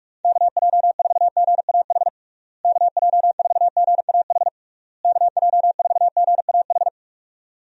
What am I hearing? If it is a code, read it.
Morse code, heard as a keyed tone, about 46 wpm, 700 Hz. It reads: KJ4GAH KJ4GAH KJ4GAH
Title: USA Call Signs - 6 Characters - Repeated 3x - 45wpm